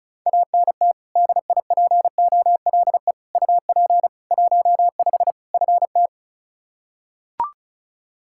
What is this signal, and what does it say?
Morse code, heard as a keyed tone, about 36 wpm, 700 Hz. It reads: ANT DIPOLE UP 15 FT
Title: Practice QSO - Exchange 2 - Info - 35wpm